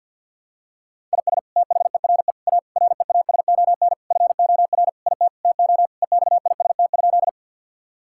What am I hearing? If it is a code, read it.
Morse code, heard as a keyed tone, about 49 wpm, 690 Hz. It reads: IS THERE A REASON FOR IT TO EXIST?
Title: Sentences from Top 1000 Words - Review Part 5 - 50wpm